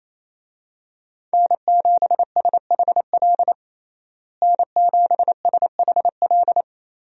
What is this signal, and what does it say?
Morse code, heard as a keyed tone, about 28 wpm, 700 Hz. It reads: N7SHL N7SHL
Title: USA Call Signs - 5 Characters - Repeated 2x - 28wpm